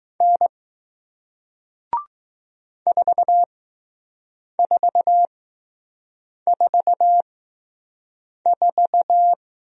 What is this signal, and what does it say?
Morse code, heard as a keyed tone, about 20 wpm, spaced as for 6 wpm, 700 Hz. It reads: N 4444